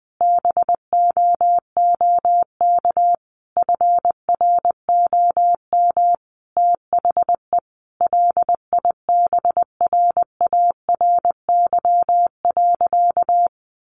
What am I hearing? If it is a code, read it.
Morse code, heard as a keyed tone, about 20 wpm, 700 Hz. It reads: BOOK FROM THE LIBRARY.
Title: Sentences from Top 1300 Words 20wpm